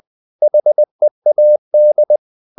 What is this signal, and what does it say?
Morse code, heard as a keyed tone, about 20 wpm, 600 Hz. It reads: HEAD
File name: the old man and the sea part 2 20wpm